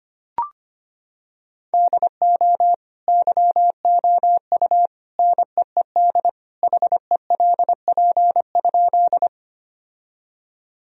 Morse code, heard as a keyed tone, about 25 wpm, 700 Hz. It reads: DO YOU NEED HELP?
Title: Sentences from Top 1000 Words - Review Part 1 - 25wpm